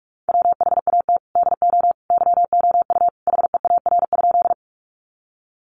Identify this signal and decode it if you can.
Morse code, heard as a keyed tone, about 44 wpm, 700 Hz. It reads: WHAT DO YOU HEAR?
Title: Sentences from Top 300 Words 45wpm